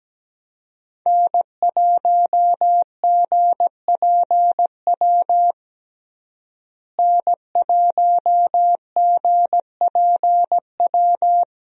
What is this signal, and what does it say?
Morse code, heard as a keyed tone, about 17 wpm, 700 Hz. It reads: N1GPW N1GPW
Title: USA Call Signs - 5 Characters - Repeated 2x - 17wpm